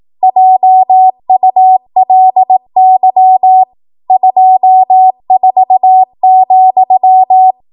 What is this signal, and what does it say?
Morse code, heard as a keyed tone, about 18 wpm, 750 Hz. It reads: JULY 24,
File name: Morse / 18WPM